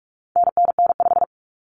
Morse code, heard as a keyed tone, about 44 wpm, 700 Hz. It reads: NNN5
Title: Common Cut Numbers - 4-Digit 45wpm